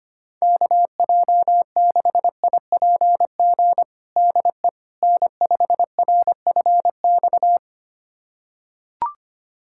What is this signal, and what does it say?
Morse code, heard as a keyed tone, about 25 wpm, 700 Hz. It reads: KJ6IPG DE N5RFX
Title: USA Callsign DE Callsign 25wpm